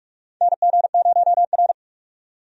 Morse code, heard as a keyed tone, about 46 wpm, 700 Hz. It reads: NG0R
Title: USA Call Signs - 4 Characters - 45wpm